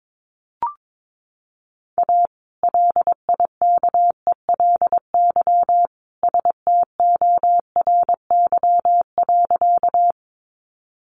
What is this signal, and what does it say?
Morse code, heard as a keyed tone, about 22 wpm, 700 Hz. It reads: A LIKELY STORY.
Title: Sentences from Top 600 Words 22wpm